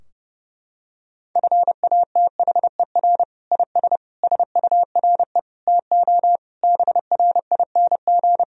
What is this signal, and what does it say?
Morse code, heard as a keyed tone, about 32 wpm, 700 Hz. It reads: FATHER IS SURE TO BRING
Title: Sentences from Top 1000 Words - Review Part 3 - 30wpm